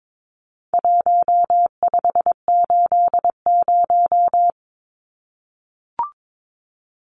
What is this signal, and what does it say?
Morse code, heard as a keyed tone, about 22 wpm, 700 Hz. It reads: 1580
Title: Numbers - Contest Style - 4-Digit 22wpm